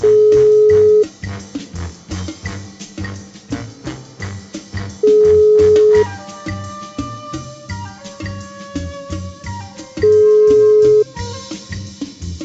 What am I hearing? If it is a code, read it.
Ringback tone (Europe, 425 Hz).